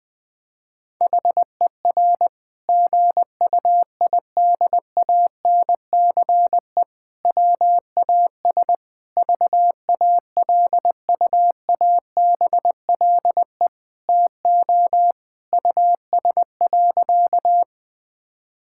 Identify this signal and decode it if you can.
Morse code, heard as a keyed tone, about 20 wpm, 700 Hz. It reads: HER GUIDANCE WAS VALUABLE TO US.
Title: Sentences from Top 1600 Words 20wpm